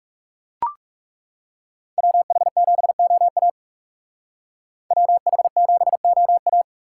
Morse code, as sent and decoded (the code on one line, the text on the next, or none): .-- .... --... --- .- / .-- .... --... --- .-
WH7OA WH7OA